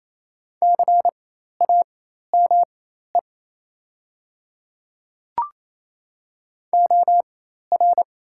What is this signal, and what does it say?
Morse code, heard as a keyed tone, about 28 wpm, spaced as for 13 wpm, 700 Hz. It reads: CAME OR
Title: 4x Character Spacing - Top 500 Words - 28wpm